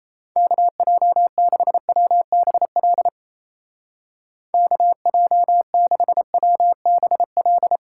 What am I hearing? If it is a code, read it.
Morse code, heard as a keyed tone, about 34 wpm, 700 Hz. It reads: KJ6WBL KJ6WBL
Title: Speed Racing - USA Call Signs 28wpm